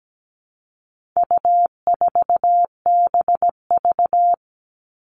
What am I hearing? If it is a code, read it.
Morse code, heard as a keyed tone, about 17 wpm, 700 Hz. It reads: U4BV
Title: All Cut Numbers - 4-Digit 17wpm